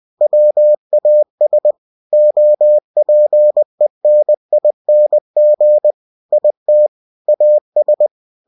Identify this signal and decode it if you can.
Morse code, heard as a keyed tone, about 20 wpm, 600 Hz. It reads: WAS OPENING IT AS
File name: cw-books / winnie-the-pooh / winnie-20wpm-page-0109